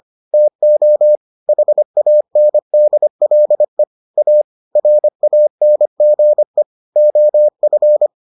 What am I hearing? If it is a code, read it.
Morse code, heard as a keyed tone, about 25 wpm, 600 Hz. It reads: TO HANDLE A RANGE OF